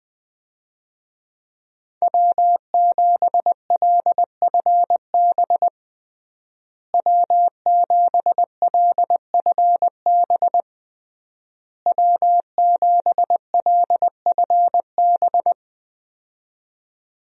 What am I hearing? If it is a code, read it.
Morse code, heard as a keyed tone, about 20 wpm, 700 Hz. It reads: W7LFB W7LFB W7LFB
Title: USA Call Signs - 5 Characters - Repeated 3x - 20wpm